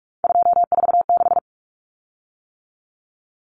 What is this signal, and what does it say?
Morse code, heard as a keyed tone, about 44 wpm, 700 Hz. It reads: 246
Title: Common Cut Numbers - 3-Digit 45wpm